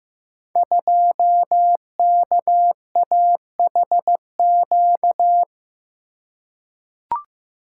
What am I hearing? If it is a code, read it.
Morse code, heard as a keyed tone, about 15 wpm, 700 Hz. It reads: 2KAHQ
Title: Code Groups - 5 Letter:Numbers - 15wpm